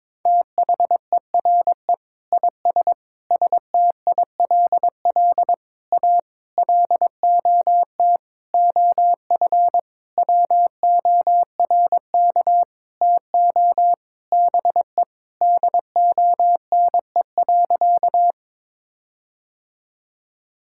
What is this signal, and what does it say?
Morse code, heard as a keyed tone, about 22 wpm, 700 Hz. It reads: THERE IS STILL A LOT OF WORK TO BE DONE.
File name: Sentences with 9-Word Phrases - 22wpm